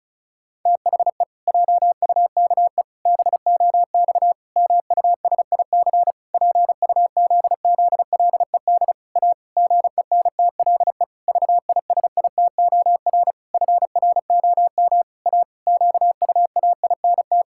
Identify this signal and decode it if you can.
Morse code, heard as a keyed tone, about 34 wpm, 700 Hz. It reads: THE JUKE BOX MUSIC PUZZLED A GENTLE VISITOR FROM A QUAINT
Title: Pangram Challenge v2 - 35wpm